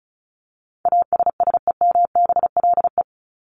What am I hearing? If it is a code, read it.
Morse code, heard as a keyed tone, about 34 wpm, 700 Hz. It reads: ASSEMBLE
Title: LY Suffix - Top 100 Words with Base Words 35wpm